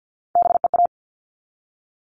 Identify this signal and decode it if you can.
Morse code, heard as a keyed tone, about 48 wpm, 700 Hz. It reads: BEA
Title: All Cut Numbers - 3-Digit 50wpm